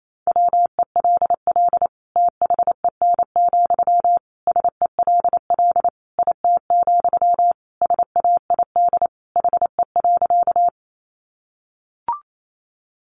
Morse code, heard as a keyed tone, about 28 wpm, 700 Hz. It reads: WELL THEN, SELL IT, SAID HE.
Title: Sentences from Top 1100 Words 28wpm